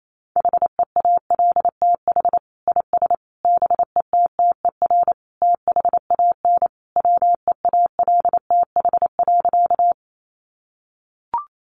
Morse code, heard as a keyed tone, about 28 wpm, 700 Hz. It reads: HEALTH IS BETTER THAN WEALTH.
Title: Sentences from Top 1200 Words 28wpm